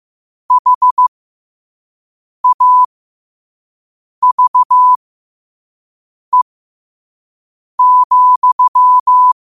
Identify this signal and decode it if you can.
Morse code, heard as a keyed tone, about 15 wpm, spaced as for 5 wpm, 1000 Hz. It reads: HAVE,
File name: qotdfiles / 5wpm